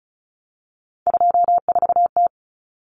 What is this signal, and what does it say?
Morse code, heard as a keyed tone, about 34 wpm, 700 Hz. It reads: 24T
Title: Common Cut Numbers - 3-Digit 35wpm